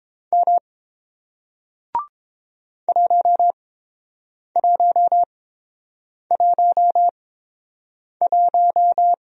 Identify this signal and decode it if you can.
Morse code, heard as a keyed tone, about 31 wpm, spaced as for 7 wpm, 700 Hz. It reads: M 1111